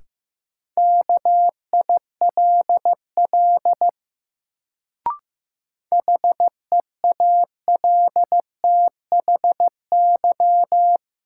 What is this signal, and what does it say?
Morse code, heard as a keyed tone, about 15 wpm, 700 Hz. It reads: KILL HEALTHY
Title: Sentences from Top 1400 Words - New Word Drill Alt - 15wpm